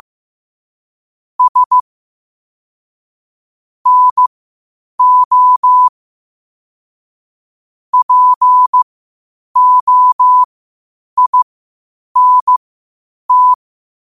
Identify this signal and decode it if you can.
Morse code, heard as a keyed tone, about 15 wpm, spaced as for 8 wpm, 1000 Hz. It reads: S NO POINT
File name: qotdfiles / 8wpm